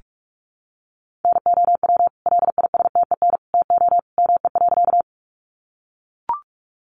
Morse code, heard as a keyed tone, about 44 wpm, 700 Hz. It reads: NOW LISTEN TO ME.
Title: Sentences from Top 400 Words 45wpm